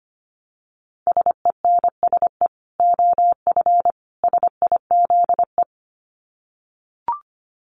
Morse code, heard as a keyed tone, about 25 wpm, 700 Hz. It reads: SENSE OF SIZE
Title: Dit Challenge 25wpm